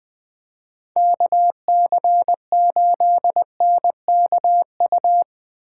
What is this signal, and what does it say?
Morse code, heard as a keyed tone, about 22 wpm, 700 Hz. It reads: KC8NKU
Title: Speed Racing - USA 2x3 Call Signs 20wpm